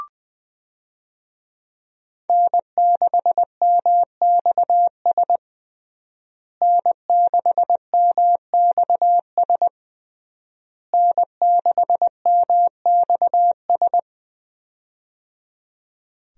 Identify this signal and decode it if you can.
Morse code, heard as a keyed tone, about 21 wpm, 700 Hz. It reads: N6MXS N6MXS N6MXS
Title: USA Call Signs - 5 Characters - Repeated 3x - 20wpm